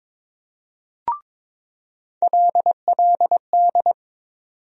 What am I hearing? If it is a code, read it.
Morse code, heard as a keyed tone, about 22 wpm, 700 Hz. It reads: LLD
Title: USA Call Sign Postfixes - 3 Letters 22wpm